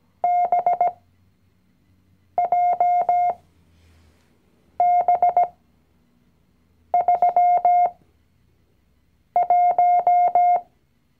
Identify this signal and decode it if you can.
Morse code, heard as a keyed tone, about 17 wpm, spaced as for 5 wpm, 710 Hz. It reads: BJB31
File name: CW-Online-Kurs / Anfänger / Lektion 019